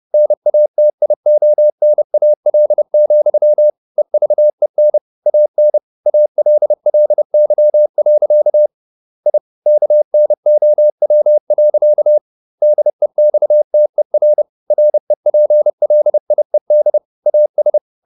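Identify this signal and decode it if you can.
Morse code, heard as a keyed tone, about 30 wpm, 600 Hz. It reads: NATIONAL, EVEN AN ALLY. I KNOW. DEXTER REPLIED AS
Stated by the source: Morse code at 30 wpm, the novel Never by Ken Follett in English, from CW med Gnister.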